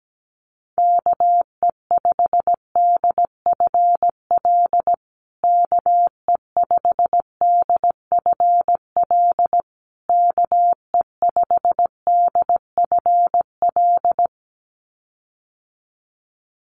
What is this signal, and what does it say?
Morse code, heard as a keyed tone, about 17 wpm, 700 Hz. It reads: KE5DFL KE5DFL KE5DFL
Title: USA Call Signs Repeated 3x 17wpm